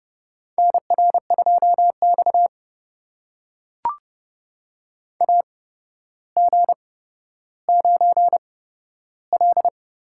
Morse code, heard as a keyed tone, about 32 wpm, spaced as for 13 wpm, 700 Hz. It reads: NR2X AG9L
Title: USA Call Signs - 4 Characters - 8x Character Spacing - 30wpm